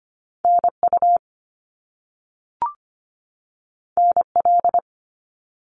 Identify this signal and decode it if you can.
Morse code, heard as a keyed tone, about 25 wpm, 700 Hz. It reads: NU NL
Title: Canadian Provinces and Territories 25wpm